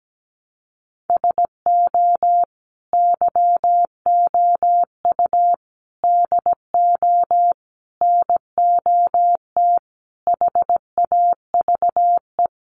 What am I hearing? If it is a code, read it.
Morse code, heard as a keyed tone, about 17 wpm, 700 Hz. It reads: SO YOU DO NOT HAVE
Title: Sets of 5 Words Encore 17wpm